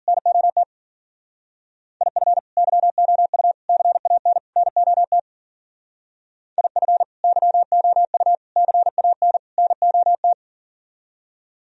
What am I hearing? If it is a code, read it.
Morse code, heard as a keyed tone, about 52 wpm, 700 Hz. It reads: NOT IF YOU CAN NOT IF YOU CAN NOT